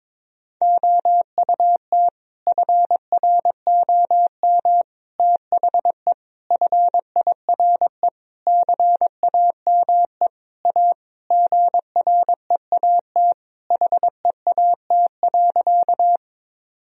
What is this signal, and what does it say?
Morse code, heard as a keyed tone, about 21 wpm, 700 Hz. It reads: OUT FROM THE FIRE CAME A GREAT HEAT.
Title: Sentences from Top 1000 Words - Review Part 3 - 22wpm